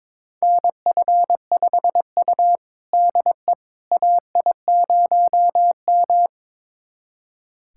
Morse code, heard as a keyed tone, about 22 wpm, 700 Hz. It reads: NF5U DE AI0M
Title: USA Callsign DE Callsign - 4 Characters - 22wpm